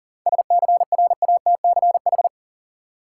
Morse code, heard as a keyed tone, about 40 wpm, 700 Hz. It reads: SCRATCH
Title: Seven Letter Words - 40wpm